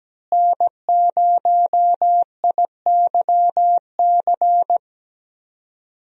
Morse code, heard as a keyed tone, about 17 wpm, 700 Hz. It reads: N0IYC
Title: USA Call Signs - 5 Characters - Repeated 3x - 17wpm